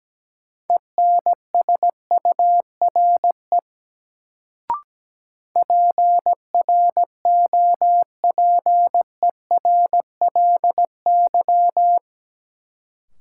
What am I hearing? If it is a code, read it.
Morse code, heard as a keyed tone, about 17 wpm, 700 Hz. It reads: ENSURE PROPERLY
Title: Sentences from Top 1100 Words - New Word Drill - 17wpm